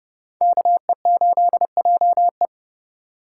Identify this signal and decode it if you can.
Morse code, heard as a keyed tone, about 38 wpm, 700 Hz. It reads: KE8JE